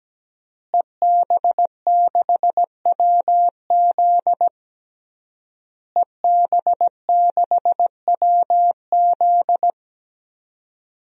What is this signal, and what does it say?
Morse code, heard as a keyed tone, about 17 wpm, 700 Hz. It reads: EB6WZ EB6WZ
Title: DXCC-100 Call Signs - 5 Characters - Repeated 3x - 17wpm